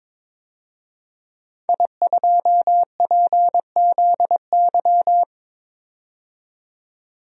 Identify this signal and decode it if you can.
Morse code, heard as a keyed tone, about 21 wpm, 700 Hz. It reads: I2PZY
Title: DXCC-100 Call Signs - 5 Characters - 22wpm